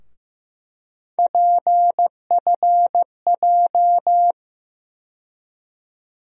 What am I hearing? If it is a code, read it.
Morse code, heard as a keyed tone, about 15 wpm, 700 Hz. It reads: PFJ
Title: USA Call Sign Postfixes - 3 Letters 15wpm